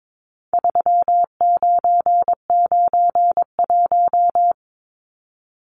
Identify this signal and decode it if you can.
Morse code, heard as a keyed tone, about 22 wpm, 700 Hz. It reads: 3991
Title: Numbers - 4-Digit 22wpm